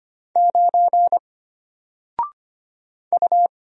Morse code, heard as a keyed tone, about 25 wpm, 700 Hz. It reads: 9 U
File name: Single-Letter-Number-25wpm